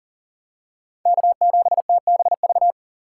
Morse code, heard as a keyed tone, about 38 wpm, 700 Hz. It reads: K7TBV
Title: USA Call Signs - 5 Characters - 3x Character Spacing - 40wpm